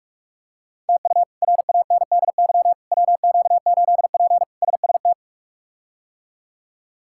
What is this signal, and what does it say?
Morse code, heard as a keyed tone, about 47 wpm, 700 Hz. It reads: TU RANDY WQ8P SST
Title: K1USN SST QSO Practice - 45wpm